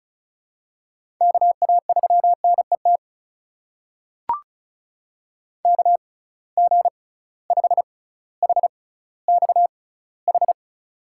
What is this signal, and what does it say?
Morse code, heard as a keyed tone, about 36 wpm, spaced as for 18 wpm, 700 Hz. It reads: KA3NET KG5HXH
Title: USA Call Signs - 6 Characters - 6x Character Spacing - 35wpm